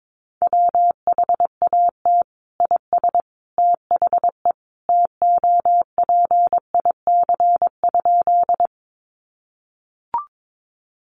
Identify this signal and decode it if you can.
Morse code, heard as a keyed tone, about 22 wpm, 700 Hz. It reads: WHAT IS THE TOPIC?
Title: Sentences from Top 1200 Words 22wpm